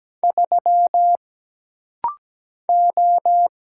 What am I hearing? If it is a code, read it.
Morse code, heard as a keyed tone, about 17 wpm, 700 Hz. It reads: 3 O
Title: Single Letter-Number Alt - Rapid-Fire 17wpm